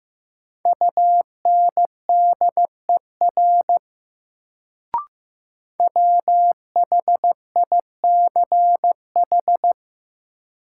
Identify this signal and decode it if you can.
Morse code, heard as a keyed tone, about 15 wpm, 700 Hz. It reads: UNDER WHICH
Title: Sets of 2 Words - New Word Drill - 15wpm